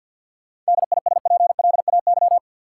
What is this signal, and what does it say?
Morse code, heard as a keyed tone, about 50 wpm, 700 Hz. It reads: DISPLAY
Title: Sentences from Top 1200 Words - New Word Drill Alt - 50wpm